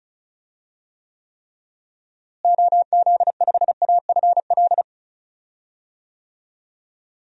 Morse code, heard as a keyed tone, about 36 wpm, 700 Hz. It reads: OZ5AFL
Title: DXCC-100 Call Signs - 6 Characters - Repeated 3x - 35wpm